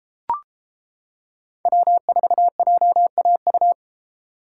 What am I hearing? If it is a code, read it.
Morse code, heard as a keyed tone, about 33 wpm, 700 Hz. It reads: W4JAU